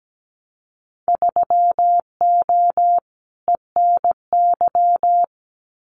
Morse code, heard as a keyed tone, about 17 wpm, 700 Hz. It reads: 3O ENY
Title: Winter Field Day Exchanges Encore 17wpm